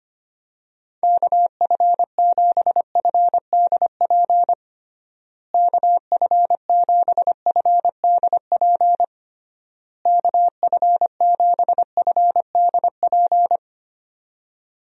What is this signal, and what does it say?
Morse code, heard as a keyed tone, about 25 wpm, 700 Hz. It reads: KF7FDP KF7FDP KF7FDP
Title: USA Call Signs - 6 Characters - Repeated 3x - 25wpm